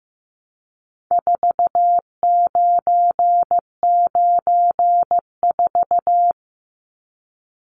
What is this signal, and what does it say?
Morse code, heard as a keyed tone, about 15 wpm, 700 Hz. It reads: 4994
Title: Numbers - 4-Digit 15wpm